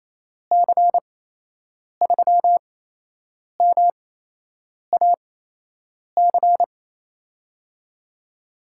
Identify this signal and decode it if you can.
Morse code, heard as a keyed tone, about 28 wpm, spaced as for 8 wpm, 700 Hz. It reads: C3MAC
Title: USA Call Signs - 6 Characters - 8x Character Spacing - 28wpm